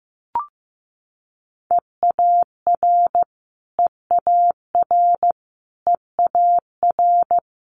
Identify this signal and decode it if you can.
Morse code, heard as a keyed tone, about 15 wpm, 700 Hz. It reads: EAR EAR EAR
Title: Multi-letter Phonemes Repeated 3x 15wpm